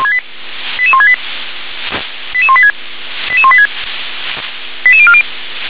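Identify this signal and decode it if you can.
Five-tone selective call (ZVEI), heard as quick runs of five tones, tones of about 70 ms. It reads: 88168 88186 88186 88040
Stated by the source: ZVEI five-tone call received with an SDR.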